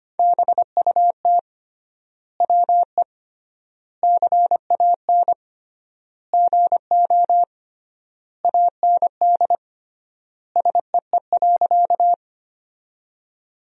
Morse code, heard as a keyed tone, about 25 wpm, 700 Hz. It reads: BUT WE CAN GO AND SEE.
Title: 3x Word-Spacing - Sentences from Top 100 Words - 25wpm